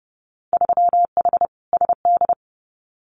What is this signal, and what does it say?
Morse code, heard as a keyed tone, about 30 wpm, 700 Hz. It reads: 3H SD
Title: Winter Field Day Exchanges Encore 30wpm